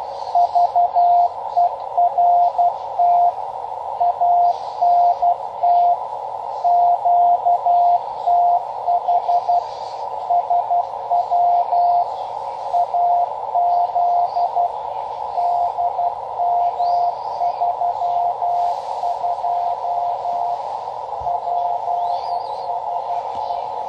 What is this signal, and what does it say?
Morse code, heard as a keyed tone, about 12 wpm, 700 Hz, with radio noise. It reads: VERT ANT QTH SW AZ DM32 PSE
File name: Half Watt 10 Meter Beacon